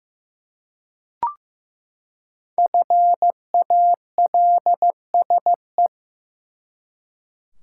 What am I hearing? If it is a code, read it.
Morse code, heard as a keyed tone, about 15 wpm, 700 Hz. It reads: FALSE
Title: Five Letter Words - 15wpm